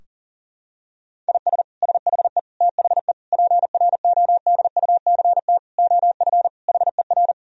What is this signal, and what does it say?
Morse code, heard as a keyed tone, about 40 wpm, 700 Hz. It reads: IS SHE THE PRODUCT OF HER
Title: Sentences from Top 1000 Words - Review Part 2 - 40wpm